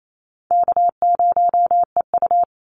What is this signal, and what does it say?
Morse code, heard as a keyed tone, about 27 wpm, 700 Hz. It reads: K0EU
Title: Easy Call Signs Repeated 2x - 28wpm